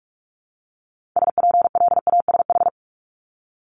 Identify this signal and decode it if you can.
Morse code, heard as a keyed tone, about 46 wpm, 700 Hz. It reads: SPLASH
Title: Multi-letter Phonemes Repeated 3x 45wpm